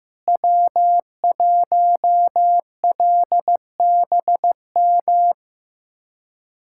Morse code, heard as a keyed tone, about 15 wpm, 700 Hz. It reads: W1LBM
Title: USA Callsign DE Callsign - 5 Characters - 15wpm